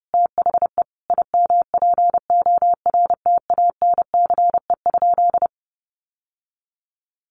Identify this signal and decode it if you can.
Morse code, heard as a keyed tone, about 30 wpm, 700 Hz. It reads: THE IMPORTANCE?
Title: Sentences from Top 1200 Words 30wpm